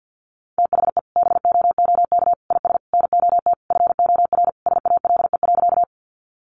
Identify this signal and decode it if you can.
Morse code, heard as a keyed tone, about 48 wpm, 700 Hz. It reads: THE BOOK IS NOT FOR SALE.
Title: Sentences from Top 1400 Words 50wpm